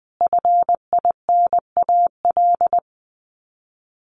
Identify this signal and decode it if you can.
Morse code, heard as a keyed tone, about 20 wpm, 700 Hz. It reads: FINAL
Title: Top 500 Words Repeated 20wpm